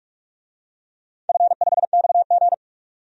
Morse code, heard as a keyed tone, about 44 wpm, 700 Hz. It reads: F5XG
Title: DXCC-100 Call Signs - 4 Characters - 45wpm